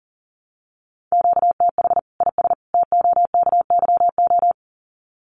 Morse code, heard as a keyed tone, about 40 wpm, 700 Hz. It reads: QTH IS TOKYO
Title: Common QSO Phrases 40wpm